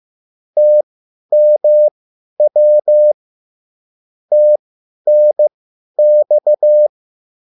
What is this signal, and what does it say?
Morse code, heard as a keyed tone, about 15 wpm, spaced as for 10 wpm, 600 Hz. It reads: TMW TNX
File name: QSL / CW lesson